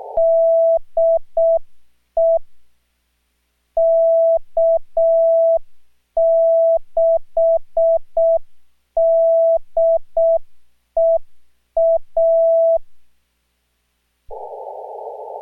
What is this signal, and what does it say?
Morse code, heard as a keyed tone, about 6 wpm, 660 Hz. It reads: DE K6DEA